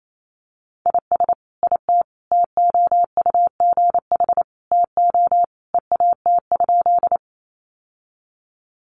Morse code, heard as a keyed tone, about 28 wpm, 700 Hz. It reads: IS IT TOUGH TO EAT?
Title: Sentences from Top 1200 Words 28wpm